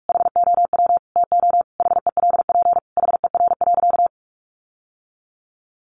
Morse code, heard as a keyed tone, about 44 wpm, 700 Hz. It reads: HOW TO HELP HER.
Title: Sentences from Top 1000 Words 45wpm